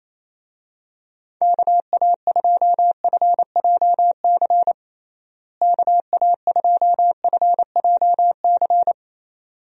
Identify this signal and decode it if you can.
Morse code, heard as a keyed tone, about 28 wpm, 700 Hz. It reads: KA2FJC KA2FJC
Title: USA Call Signs - 6 Characters - Repeated 2x - 28wpm